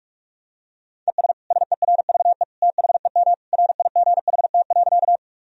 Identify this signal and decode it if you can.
Morse code, heard as a keyed tone, about 43 wpm, 700 Hz. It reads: ES SERVE THEM RIGHT.
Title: Sentences from Top 400 Words and Abbreviations 45wpm